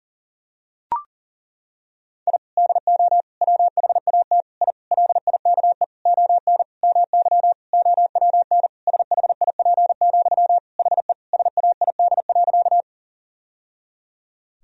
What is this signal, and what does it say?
Morse code, heard as a keyed tone, about 37 wpm, 700 Hz. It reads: I DO WHAT I LIKE ON MY OWN SHIP, HE SAID.
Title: Sentences from Top 1000 Words - Review Part 2 - 40wpm